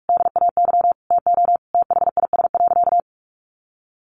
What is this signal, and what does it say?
Morse code, heard as a keyed tone, about 44 wpm, 700 Hz. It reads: DAY TO THIS.
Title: Sentences from Top 1400 Words 45wpm